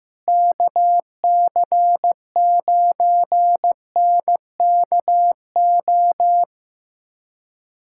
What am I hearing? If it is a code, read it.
Morse code, heard as a keyed tone, about 15 wpm, 700 Hz. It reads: KC9NKO
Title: USA Callsign DE Callsign - 6 Characters - 15wpm